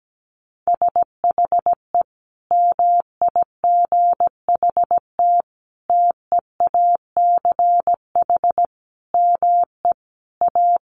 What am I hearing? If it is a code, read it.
Morse code, heard as a keyed tone, about 17 wpm, 700 Hz. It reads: SHE MIGHT TEACH ME A
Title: Sentences from Top 1200 Words 17wpm